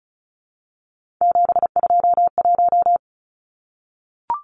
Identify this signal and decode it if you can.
Morse code, heard as a keyed tone, about 34 wpm, 700 Hz. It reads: 721
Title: Numbers - 3-Digit 35wpm